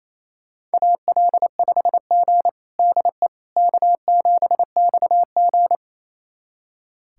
Morse code, heard as a keyed tone, about 28 wpm, 700 Hz. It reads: AL5G DE K7XG
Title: USA Callsign DE Callsign - 4 Characters - 28wpm